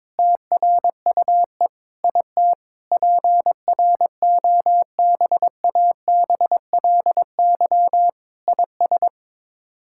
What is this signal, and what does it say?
Morse code, heard as a keyed tone, about 22 wpm, 700 Hz. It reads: TRUE IT PROBABLY IS
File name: Sets of 9 Words - 22wpm